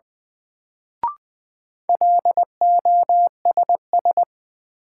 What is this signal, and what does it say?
Morse code, heard as a keyed tone, about 21 wpm, 700 Hz. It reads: LOSS
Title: Sentences from Top 1400 Words - New Word Drill - 20wpm